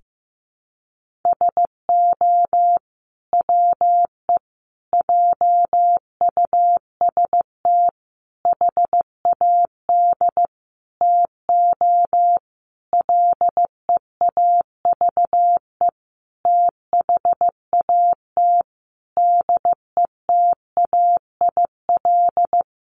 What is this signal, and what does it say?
Morse code, heard as a keyed tone, about 15 wpm, 700 Hz. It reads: SO WE JUST HAD TO LEAVE THAT DETAIL
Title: Sentences from Top 1500 Words 15wpm